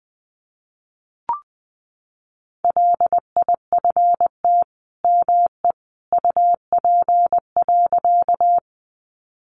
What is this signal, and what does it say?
Morse code, heard as a keyed tone, about 20 wpm, 700 Hz. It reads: LIFT ME UP.
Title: Sentences from Top 1000 Words 20wpm